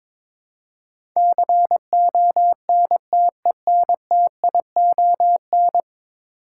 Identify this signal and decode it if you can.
Morse code, heard as a keyed tone, about 22 wpm, 700 Hz. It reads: CONTENTION
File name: practice / 22wpm / 10 Letter Words - First 3 Letters Given - 22wpm